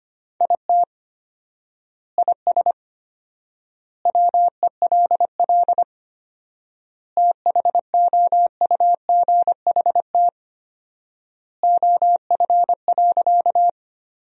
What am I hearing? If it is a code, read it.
Morse code, heard as a keyed tone, about 25 wpm, 700 Hz. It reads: IT IS WELL THOUGHT OF.
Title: 4x Word-Spacing - Sentences from Top 300 Words - 25wpm